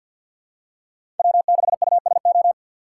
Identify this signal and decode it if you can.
Morse code, heard as a keyed tone, about 53 wpm, 700 Hz. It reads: W6USO
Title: USA Call Signs - 5 Characters - 6x Character Spacing - 50wpm